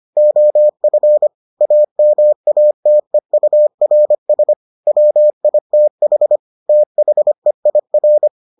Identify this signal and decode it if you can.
Morse code, heard as a keyed tone, about 25 wpm, 600 Hz. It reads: OF AMATEURS WITH THEIR